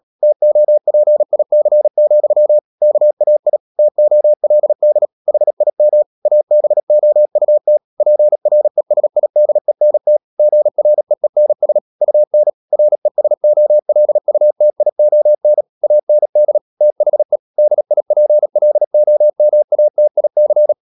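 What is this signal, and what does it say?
Morse code, heard as a keyed tone, about 37 wpm, 600 Hz. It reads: TOPIC, KAI TOLD HIM ABOUT PRESIDENT GREENS UN RESOLUTION AND THE DIPLOMATIC
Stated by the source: Morse code practice at 35–39 wpm sending English text of Ken Follett's novel Never, from CW med Gnister.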